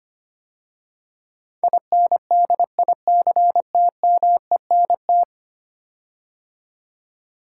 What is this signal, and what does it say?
Morse code, heard as a keyed tone, about 25 wpm, 700 Hz. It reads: INDICTMENT
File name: 10 Letter Words - First 3 Letters Given - 25wpm